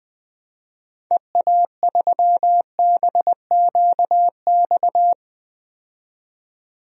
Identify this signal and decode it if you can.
Morse code, heard as a keyed tone, about 20 wpm, 700 Hz. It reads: EA3BQX
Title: DXCC-100 Call Signs - 6 Characters - 20wpm